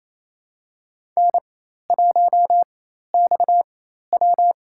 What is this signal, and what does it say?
Morse code, heard as a keyed tone, about 28 wpm, spaced as for 13 wpm, 700 Hz. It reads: N1XW
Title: USA Call Signs - 4 Characters - 4x Character Spacing - 28wpm